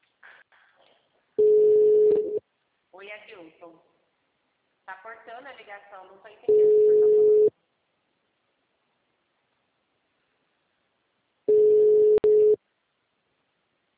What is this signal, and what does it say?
Ringback tone (Europe, 425 Hz), with speech.